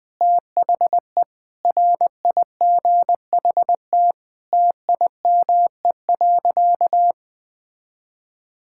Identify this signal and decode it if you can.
Morse code, heard as a keyed tone, about 20 wpm, 700 Hz. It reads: THE RIGHT TIME.
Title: Sentences with 9-Word Phrases - 20wpm